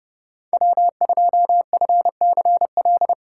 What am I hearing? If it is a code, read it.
Morse code, heard as a keyed tone, about 30 wpm, 700 Hz. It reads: W2FCL
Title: USA Callsign DE Callsign 30wpm